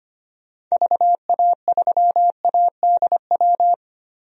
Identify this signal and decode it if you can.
Morse code, heard as a keyed tone, about 25 wpm, 700 Hz. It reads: VA3ADW
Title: DXCC-100 Call Signs - 6 Characters - 25wpm